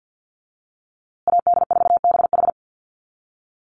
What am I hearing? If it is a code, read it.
Morse code, heard as a keyed tone, about 50 wpm, 700 Hz. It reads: AD4BH
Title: USA Call Signs Repeated 2x 50wpm